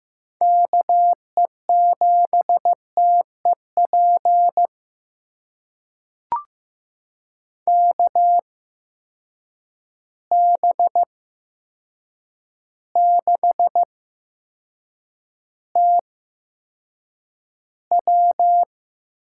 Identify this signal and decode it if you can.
Morse code, heard as a keyed tone, about 15 wpm, spaced as for 7 wpm, 700 Hz. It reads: KE7TEP KB6TW
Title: USA Call Signs - 6 Characters - 8x Character Spacing - 15wpm